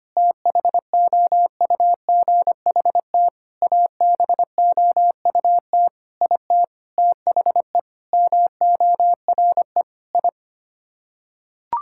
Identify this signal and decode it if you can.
Morse code, heard as a keyed tone, about 25 wpm, 700 Hz. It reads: THOUGHT ABOUT IT THE MORE I
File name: Sets of 9 Words - 25wpm